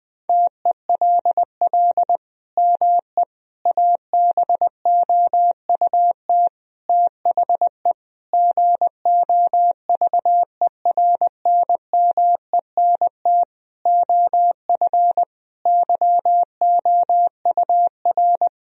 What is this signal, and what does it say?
Morse code, heard as a keyed tone, about 21 wpm, 700 Hz. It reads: TELL ME ABOUT THE GOVERNMENT OF YOUR
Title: Sentences from Top 1000 Words - Review Part 3 - 20wpm